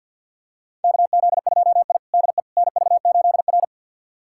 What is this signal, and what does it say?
Morse code, heard as a keyed tone, about 52 wpm, 700 Hz. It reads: KZ2I DE NV8R
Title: USA Callsign DE Callsign - 4 Characters - 50wpm